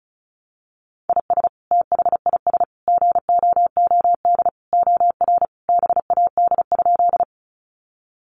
Morse code, heard as a keyed tone, about 34 wpm, 700 Hz. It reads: IS THIS GOOD OR BAD?
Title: Sentences from Top 700 Words 35wpm